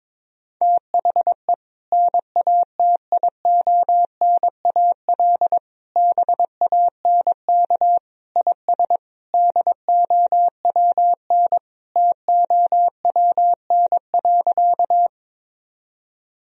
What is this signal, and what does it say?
Morse code, heard as a keyed tone, about 22 wpm, 700 Hz. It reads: THE NATIONAL BANK IS DOWN TOWN.
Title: Sentences from Top 1000 Words - Review Part 4 - 22wpm